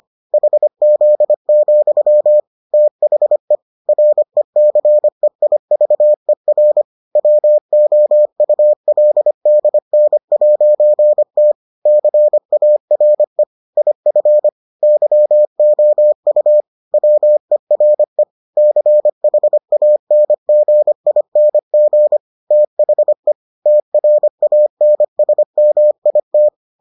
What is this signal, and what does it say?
Morse code, heard as a keyed tone, about 25 wpm, 600 Hz. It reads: HZ, THE RECEIVER WOULDN'T CARE IF YOU WERE CHANGING THE TRANSMIT